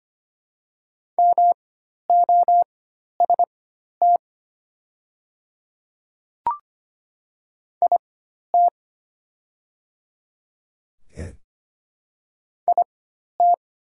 Morse code, heard as a keyed tone, about 25 wpm, spaced as for 12 wpm, 700 Hz. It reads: MOST IT IT
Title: 4x Character Spacing - Top 100 Words - 25wpm